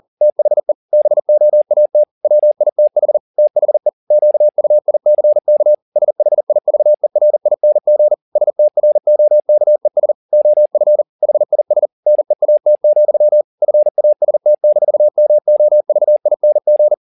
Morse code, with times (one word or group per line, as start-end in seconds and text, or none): 0.2-0.7 THE
0.9-2.0 BOAT
2.2-3.2 WITH
3.4-3.9 THE
4.1-5.7 QUICK
6.0-8.1 SHIVERING
8.4-10.1 STROKES
10.3-11.0 OF
11.2-11.9 HIS
12.1-13.4 NEAT,
13.6-16.9 FAST-MOVING